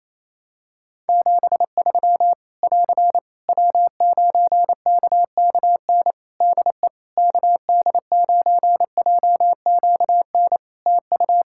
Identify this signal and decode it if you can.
Morse code, heard as a keyed tone, about 28 wpm, 700 Hz. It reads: 73 <AR> W9KKN DE KD9JQN TU
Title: Practice QSO - Exchange 4 - Ending - 28wpm